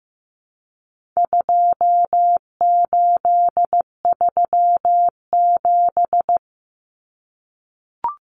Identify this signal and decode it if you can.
Morse code, heard as a keyed tone, about 15 wpm, 700 Hz. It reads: 2837
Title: Common Cut Numbers - 4-Digit 15wpm